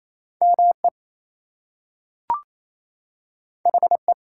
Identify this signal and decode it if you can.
Morse code, heard as a keyed tone, about 28 wpm, 700 Hz. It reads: ME HE
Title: Two Letter Words - 28wpm